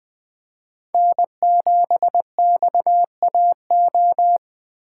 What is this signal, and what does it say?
Morse code, heard as a keyed tone, about 20 wpm, 700 Hz. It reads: N7XAO
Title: Speed Racing - USA 1x3 Call Signs 17wpm